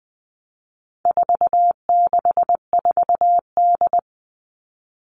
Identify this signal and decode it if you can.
Morse code, heard as a keyed tone, about 20 wpm, 700 Hz. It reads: 464D
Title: All Cut Numbers - 4-Digit 20wpm